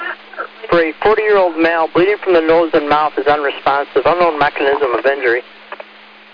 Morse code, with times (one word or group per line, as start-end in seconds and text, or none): none